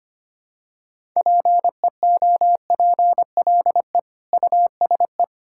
Morse code, heard as a keyed tone, about 25 wpm, 700 Hz. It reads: PEOPLE USE
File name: practice / 25wpm / Difficult Sentences with Hints - 25wpm